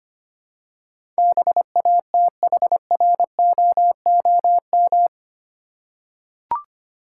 Morse code, heard as a keyed tone, about 25 wpm, 700 Hz. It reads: BATHROOM
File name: 8 Letter Words - First 3 Letters Given - 25wpm